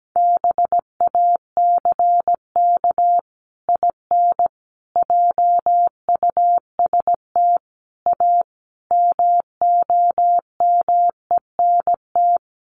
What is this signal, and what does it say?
Morse code, heard as a keyed tone, about 17 wpm, 700 Hz. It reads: BACK IN JUST A MOMENT
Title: Sets of 5 Words Encore 17wpm